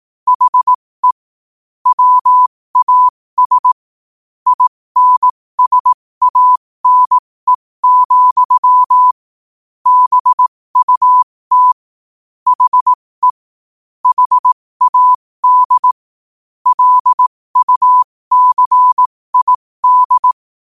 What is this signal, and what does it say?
Morse code, heard as a keyed tone, about 18 wpm, spaced as for 15 wpm, 1000 Hz. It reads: HE WAS INSANE, BUT HE HAD LUCID
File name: qotdfiles / 15wpm